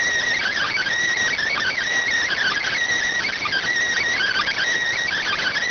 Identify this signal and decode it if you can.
Emergency Alert System (SAME) tones (header bursts), heard as harsh buzzing bursts. It is heard from 0.3 to 4.6 s.
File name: findings / MR 2 Pel Sag T2 SSFSE 0009.gen-38 rx